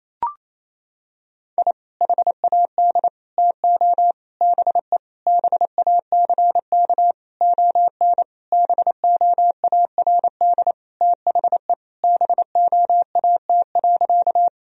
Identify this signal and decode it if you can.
Morse code, heard as a keyed tone, about 28 wpm, 700 Hz. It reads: I HAD TO BE BACK ON BOARD THE BOAT.
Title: Sentences from Top 1000 Words - Review Part 4 - 28wpm